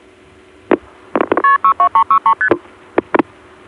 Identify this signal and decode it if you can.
Multi-frequency trunk signalling, heard as fast pairs of tones, tones of about 90 ms. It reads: KP62363ST